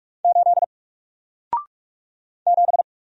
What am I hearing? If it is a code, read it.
Morse code, heard as a keyed tone, about 44 wpm, 700 Hz. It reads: 8 7
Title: Single Letter-Number Alt 45wpm